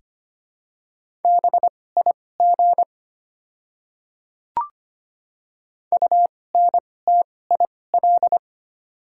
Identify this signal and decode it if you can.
Morse code, heard as a keyed tone, about 25 wpm, spaced as for 18 wpm, 700 Hz. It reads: BIG UNTIL
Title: 2x Character Spacing - Top 500 Words - 25wpm